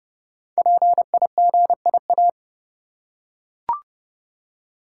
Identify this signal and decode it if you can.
Morse code, heard as a keyed tone, about 29 wpm, 700 Hz. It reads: PIGIA
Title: Code Groups - 5 Letters - 30wpm